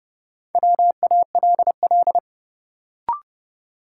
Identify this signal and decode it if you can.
Morse code, heard as a keyed tone, about 30 wpm, 700 Hz. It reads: WALL
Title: Sentences from Top 1000 Words - Review Part 6 - New Word Drill - 30wpm